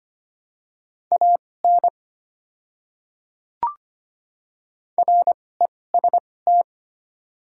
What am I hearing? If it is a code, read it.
Morse code, heard as a keyed tone, about 25 wpm, spaced as for 18 wpm, 700 Hz. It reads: AN REST
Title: 2x Character Spacing - Top 500 Words - 25wpm